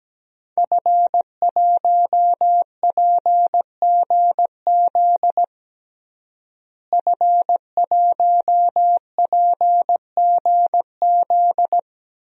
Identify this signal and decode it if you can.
Morse code, heard as a keyed tone, about 17 wpm, 700 Hz. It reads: F1PGZ F1PGZ
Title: DXCC-100 Call Signs - 5 Characters - Repeated 2x - 17wpm